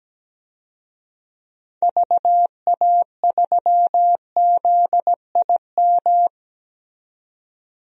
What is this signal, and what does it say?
Morse code, heard as a keyed tone, about 17 wpm, 700 Hz. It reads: VA3ZIM
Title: DXCC-100 Call Signs - 6 Characters - 17wpm